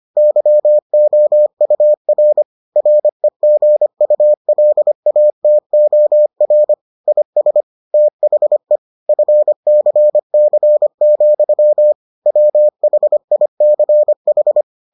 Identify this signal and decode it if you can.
Morse code, heard as a keyed tone, about 25 wpm, 600 Hz. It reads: YOUR REGULATOR IS THE FCC, WHICH